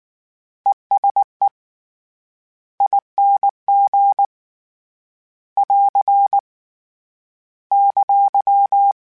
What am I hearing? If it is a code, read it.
Morse code, heard as a keyed tone, about 19 wpm, 800 Hz. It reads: ESE ING <AR> !
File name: Short Runs / Short Runs-mp3s / ESE ING @18